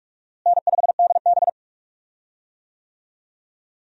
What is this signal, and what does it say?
Morse code, heard as a keyed tone, about 46 wpm, 700 Hz. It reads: N5DB
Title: USA Call Signs - 4 Characters - Repeated 3x - 45wpm